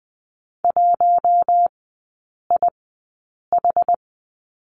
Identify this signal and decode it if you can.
Morse code, heard as a keyed tone, about 20 wpm, spaced as for 8 wpm, 700 Hz. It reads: 1IH
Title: 060 Lesson 11e - Letter h - Three Characters 20wpm